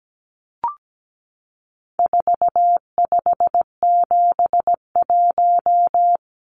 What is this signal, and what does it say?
Morse code, heard as a keyed tone, about 17 wpm, 700 Hz. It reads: 4571
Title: Numbers - 4-Digit 17wpm